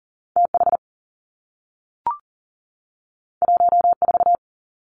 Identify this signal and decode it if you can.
Morse code, heard as a keyed tone, about 40 wpm, 700 Hz. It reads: TH 14
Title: Two Challenging Letter-Number Combinations 40wpm